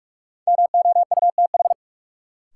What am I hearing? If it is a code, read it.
Morse code, heard as a keyed tone, about 44 wpm, 700 Hz. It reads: MOUTH
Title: Sentences from Top 1500 Words - New Word Drill - 45wpm